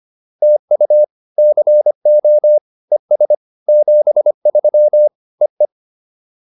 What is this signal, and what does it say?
Morse code, heard as a keyed tone, about 25 wpm, 600 Hz. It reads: TU CO ES 73 EE